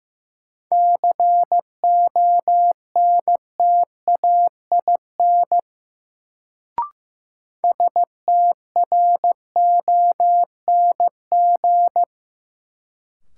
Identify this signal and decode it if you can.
Morse code, heard as a keyed tone, about 15 wpm, 700 Hz. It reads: CONTAIN STRONG
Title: Sentences from Top 500 Words - New Word Drill - 15wpm